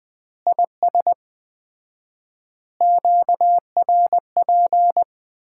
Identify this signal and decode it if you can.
Morse code, heard as a keyed tone, about 20 wpm, 700 Hz. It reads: IS QRP
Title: Common QSO Phrases - 4x Word-Spacing - 20wpm